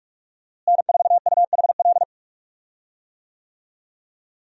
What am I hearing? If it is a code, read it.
Morse code, heard as a keyed tone, about 46 wpm, 700 Hz. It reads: N4UHL
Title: USA Call Signs - 5 Characters - Repeated 3x - 45wpm